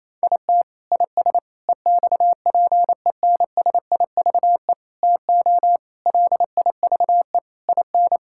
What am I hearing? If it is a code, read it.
Morse code, heard as a keyed tone, about 28 wpm, 700 Hz. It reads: IT IS EXPENSIVE TO LIVE IN
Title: Sentences from Top 1000 Words - Review Part 6 - 28wpm